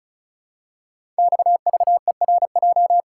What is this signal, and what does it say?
Morse code, heard as a keyed tone, about 36 wpm, 700 Hz. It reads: XVERJ
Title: Code Groups - 5 Letters - 35wpm